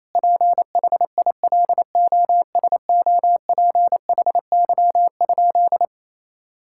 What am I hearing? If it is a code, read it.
Morse code, heard as a keyed tone, about 28 wpm, 700 Hz. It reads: PHILOSOPHY?
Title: Sentences from Top 1000 Words - Review Part 6 - 28wpm